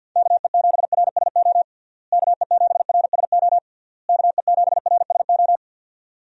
Morse code, heard as a keyed tone, about 53 wpm, 700 Hz. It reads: KE7RSO KE7RSO KE7RSO
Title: USA Call Signs - 6 Characters - Repeated 3x - 50wpm